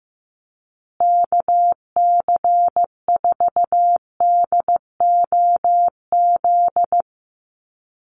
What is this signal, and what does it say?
Morse code, heard as a keyed tone, about 15 wpm, 700 Hz. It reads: KC4DOZ
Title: USA Call Signs Repeated 3x 15wpm